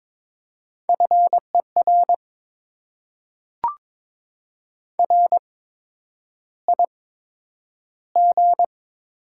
Morse code, heard as a keyed tone, about 22 wpm, spaced as for 9 wpm, 700 Hz. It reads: FER RIG
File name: Top 20 QSO Elements - 8x Character Spacing - 22wpm